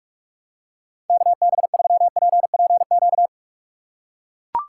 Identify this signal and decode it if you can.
Morse code, heard as a keyed tone, about 44 wpm, 700 Hz. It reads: KB3PPQ
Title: USA Call Signs - 3x Character Spacing - 45wpm